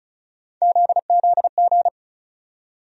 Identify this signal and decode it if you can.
Morse code, heard as a keyed tone, about 36 wpm, 700 Hz. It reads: ZZG
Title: USA Call Sign Postfixes - 3 Letters 35wpm